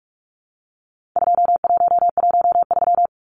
Morse code, heard as a keyed tone, about 44 wpm, 700 Hz. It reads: 2113
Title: Numbers - 4-Digit 45wpm